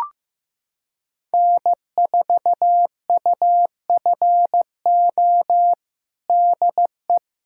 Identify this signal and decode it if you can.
Morse code, heard as a keyed tone, about 15 wpm, 700 Hz. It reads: N4UFO DE
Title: USA Callsign DE Callsign - 5 Characters - 15wpm